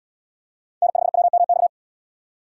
Morse code, heard as a keyed tone, about 73 wpm, 690 Hz. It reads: N5WMX